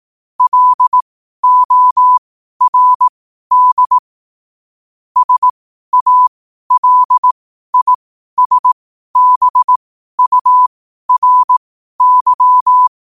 Morse code, heard as a keyed tone, about 18 wpm, spaced as for 12 wpm, 1000 Hz. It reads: LORD SALISBURY
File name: qotdfiles / 12wpm